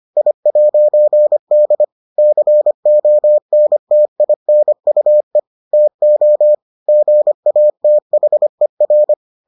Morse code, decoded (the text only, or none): I'D CONTINUE TO GATHER